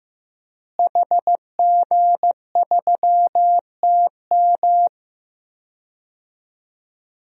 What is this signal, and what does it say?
Morse code, heard as a keyed tone, about 15 wpm, 700 Hz. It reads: HG3TM
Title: DXCC-100 Call Signs - 5 Characters - Repeated 2x - 15wpm